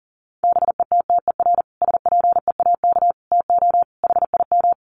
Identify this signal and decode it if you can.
Morse code, heard as a keyed tone, about 40 wpm, 700 Hz. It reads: BETTER SPEAK TO HIM
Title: Sentences from Top 1100 Words 40wpm